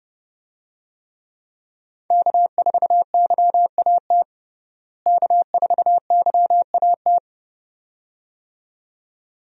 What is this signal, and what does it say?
Morse code, heard as a keyed tone, about 31 wpm, 700 Hz. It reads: K4YAT K4YAT
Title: USA Call Signs - 5 Characters - Repeated 2x - 30wpm